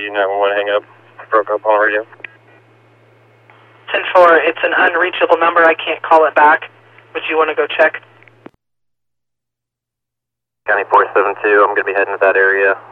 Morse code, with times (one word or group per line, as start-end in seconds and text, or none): none